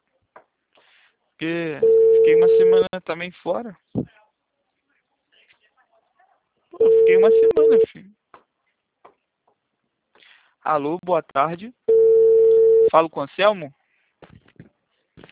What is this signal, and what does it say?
Ringback tone (China, 450 Hz), with speech.